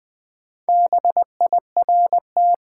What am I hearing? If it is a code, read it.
Morse code, heard as a keyed tone, about 20 wpm, 700 Hz. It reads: BIRT
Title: Build A Word - 2 Letters At A Time - 8-Letter Words - 20wpm